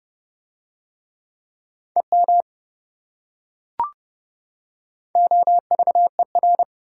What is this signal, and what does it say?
Morse code, heard as a keyed tone, about 29 wpm, 700 Hz. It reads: EM OVER